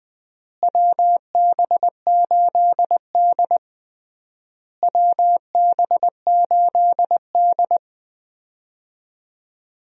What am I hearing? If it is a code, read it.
Morse code, heard as a keyed tone, about 20 wpm, 700 Hz. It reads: WB8D WB8D
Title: USA Call Signs - 4 Characters - Repeated 3x - 20wpm